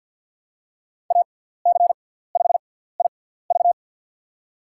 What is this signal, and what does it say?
Morse code, heard as a keyed tone, about 46 wpm, spaced as for 17 wpm, 690 Hz. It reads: AC5IV
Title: USA Call Signs - 6x Character Spacing - 50wpm